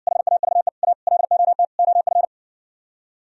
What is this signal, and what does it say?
Morse code, heard as a keyed tone, about 60 wpm, 700 Hz. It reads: HAVE A LOT OF